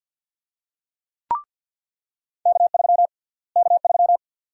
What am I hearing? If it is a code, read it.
Morse code, heard as a keyed tone, about 50 wpm, 700 Hz. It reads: K3 K3
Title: USA Call Sign Prefixes - Repeated 2x - 50wpm